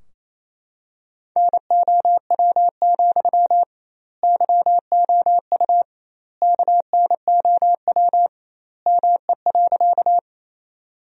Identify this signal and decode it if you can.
Morse code, heard as a keyed tone, about 28 wpm, 700 Hz. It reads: NOW, YOU KNOW ME.
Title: 2x Word-Spacing - Sentences from Top 200 Words - 28wpm